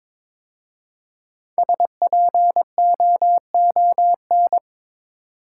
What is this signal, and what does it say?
Morse code, heard as a keyed tone, about 22 wpm, 700 Hz. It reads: SPOON
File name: 5 Letter Words - First 3 Letters Given - 22wpm